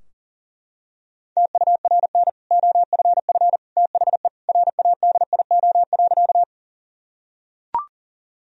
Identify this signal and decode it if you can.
Morse code, heard as a keyed tone, about 40 wpm, 700 Hz. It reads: TURN OFF THE RADIO.
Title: Sentences from Top 1000 Words - Review Part 3 - 40wpm